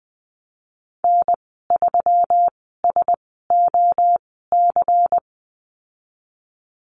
Morse code, heard as a keyed tone, about 20 wpm, spaced as for 14 wpm, 700 Hz. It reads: N3SOC
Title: 147 Lesson 24g - Letter y - Call Signs 20wpm